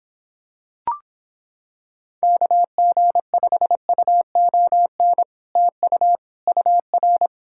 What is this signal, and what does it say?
Morse code, heard as a keyed tone, about 26 wpm, 700 Hz. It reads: KG5UON TU UR